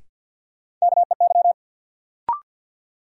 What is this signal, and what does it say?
Morse code, heard as a keyed tone, about 52 wpm, 700 Hz. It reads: KEY
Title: Sentences from Top 600 Words - New Word Drill - 50wpm